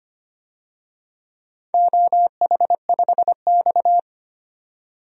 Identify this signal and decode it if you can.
Morse code, heard as a keyed tone, about 25 wpm, 700 Hz. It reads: OH5X
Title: DXCC-100 Call Signs - 4 Characters - 25wpm